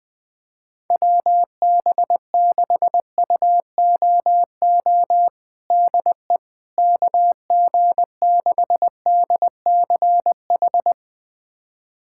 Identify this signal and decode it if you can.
Morse code, heard as a keyed tone, about 20 wpm, 700 Hz. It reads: WB6UOO DE KG6DCH
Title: USA Callsign DE Callsign 20wpm